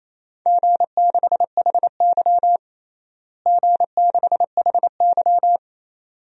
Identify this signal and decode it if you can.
Morse code, heard as a keyed tone, about 28 wpm, 700 Hz. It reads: G6HY G6HY
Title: DXCC-100 Call Signs - 4 Characters - Repeated 2x - 28wpm